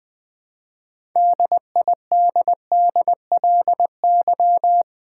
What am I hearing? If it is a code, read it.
Morse code, heard as a keyed tone, about 20 wpm, 700 Hz. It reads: DIDDLY
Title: All-Age Appropriate Jokes - 20wpm